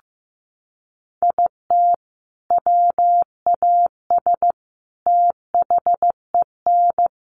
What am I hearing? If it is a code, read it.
Morse code, heard as a keyed tone, about 15 wpm, 700 Hz. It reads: IT WAS THEN
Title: Sentences from Top 300 Words 15wpm